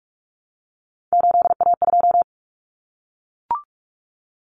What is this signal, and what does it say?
Morse code, heard as a keyed tone, about 44 wpm, 700 Hz. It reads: 8A2
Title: Common Cut Numbers - 3-Digit 45wpm